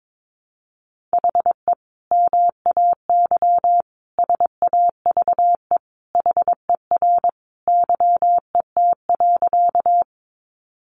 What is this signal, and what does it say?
Morse code, heard as a keyed tone, about 22 wpm, 700 Hz. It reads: HE MAY SAVE HER YET.
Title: Sentences from Top 1100 Words 22wpm